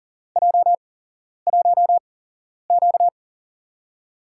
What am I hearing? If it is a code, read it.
Morse code, heard as a keyed tone, about 40 wpm, spaced as for 11 wpm, 700 Hz. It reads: J1Q
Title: USA Call Signs - 4 Characters - 8x Character Spacing - 40wpm